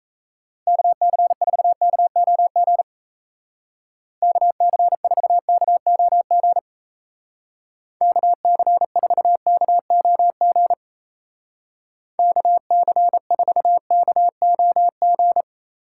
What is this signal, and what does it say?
Morse code, heard as a keyed tone, about 36 wpm, 700 Hz. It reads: KC4KOG KC4KOG KC4KOG KC4KOG